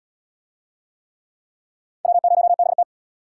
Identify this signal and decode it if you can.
Morse code, heard as a keyed tone, about 66 wpm, 700 Hz. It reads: W0BT